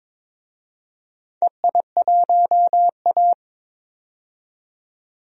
Morse code, heard as a keyed tone, about 22 wpm, 700 Hz. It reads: EI1A
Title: DXCC-100 Call Signs - 4 Characters - 22wpm